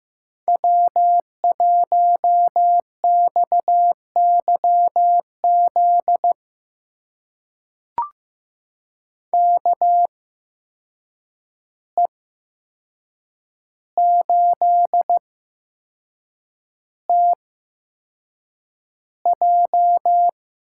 Morse code, heard as a keyed tone, about 15 wpm, spaced as for 6 wpm, 700 Hz. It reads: W1XYZ KE8TJ
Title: USA Call Signs - 5 Characters - 8x Character Spacing - 15wpm